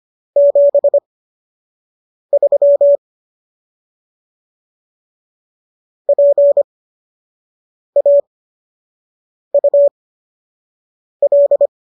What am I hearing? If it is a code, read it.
Morse code, heard as a keyed tone, about 25 wpm, spaced as for 6 wpm, 570 Hz. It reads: 73 PAUL